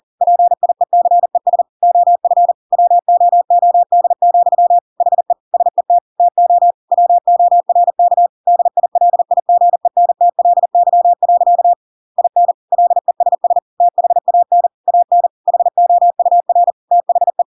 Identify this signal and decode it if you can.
Morse code, heard as a keyed tone, about 40 wpm, 700 Hz. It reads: PIECES OF WOOD, HE SET TO WORK DILIGENTLY. IN LESS THAN AN HOUR THE
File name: Pinocchio / Pinocchio ch8 40 40 ik1hge